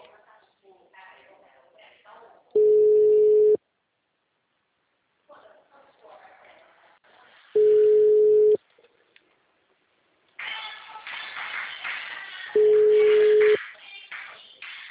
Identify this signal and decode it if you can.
Ringback tone (Europe, 425 Hz), with speech.
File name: out- -2009- - - .47462